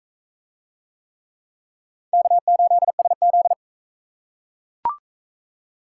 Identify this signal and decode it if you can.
Morse code, heard as a keyed tone, about 39 wpm, 700 Hz. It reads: K8SZ